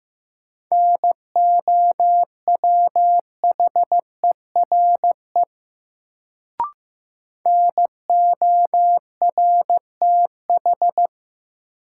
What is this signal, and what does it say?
Morse code, heard as a keyed tone, about 15 wpm, 700 Hz. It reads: NOWHERE NORTH
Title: Sets of 5 Words Encore - New Word Drill - 15wpm